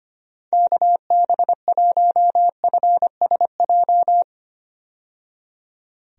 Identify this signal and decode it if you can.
Morse code, heard as a keyed tone, about 25 wpm, 700 Hz. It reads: KB1FSJ
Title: USA Call Signs - 6 Characters - 25wpm